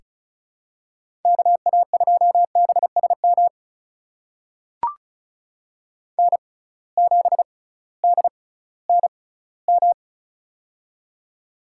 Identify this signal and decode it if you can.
Morse code, heard as a keyed tone, about 36 wpm, spaced as for 19 wpm, 700 Hz. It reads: KA2BSM N7DNM
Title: USA Call Signs - 6x Character Spacing - 35wpm